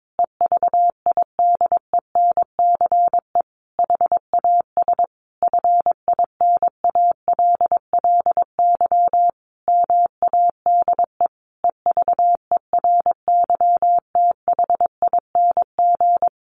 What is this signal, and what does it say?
Morse code, heard as a keyed tone, about 22 wpm, 700 Hz. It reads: EVIDENCE HAS FINALLY MADE EVERYTHING
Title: Sentences from Top 1300 Words 22wpm